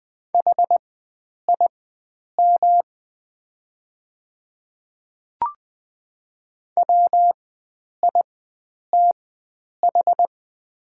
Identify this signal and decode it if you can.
Morse code, heard as a keyed tone, about 20 wpm, spaced as for 9 wpm, 690 Hz. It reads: HIM WITH
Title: 4x Character Spacing - Top 100 Words - 20wpm